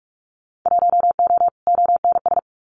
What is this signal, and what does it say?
Morse code, heard as a keyed tone, about 44 wpm, 700 Hz. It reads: 1O ONS
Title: Winter Field Day Exchanges 45wpm